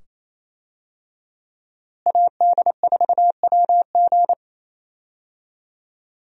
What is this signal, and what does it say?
Morse code, heard as a keyed tone, about 28 wpm, 700 Hz. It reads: AD4WG
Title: USA Call Signs - 5 Characters - Repeated 2x - 28wpm